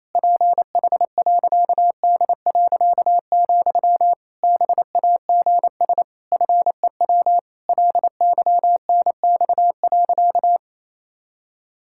Morse code, heard as a keyed tone, about 28 wpm, 700 Hz. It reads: PH.D., BAGS FEW LYNX.
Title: Pangram Challenge v2 - 28wpm